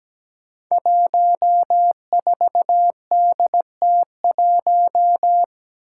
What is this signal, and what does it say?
Morse code, heard as a keyed tone, about 17 wpm, 700 Hz. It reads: 14DT1
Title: Five Challenging Letter-Number Combinations 17wpm